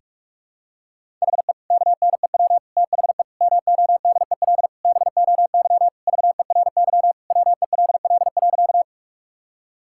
Morse code, heard as a keyed tone, about 47 wpm, 700 Hz. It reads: HE KNEW THE MODEL BOY VERY WELL.
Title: Sentences from Top 1000 Words - Review Part 5 - 45wpm